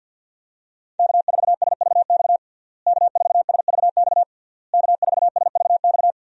Morse code, heard as a keyed tone, about 51 wpm, 700 Hz. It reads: K4SVX K4SVX K4SVX
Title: USA Call Signs - 5 Characters - Repeated 3x - 50wpm